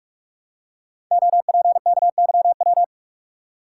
Morse code, heard as a keyed tone, about 44 wpm, 700 Hz. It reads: OPKYW
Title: Code Groups - 5 Letter:Numbers - 45wpm